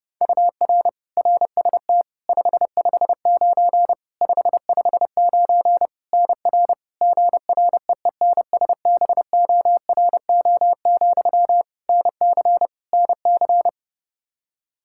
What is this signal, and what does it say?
Morse code, heard as a keyed tone, about 30 wpm, 700 Hz. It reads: UR RST 559 559 NR GREENSBORO, NC NC
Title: K1USN SST QSO Practice 30wpm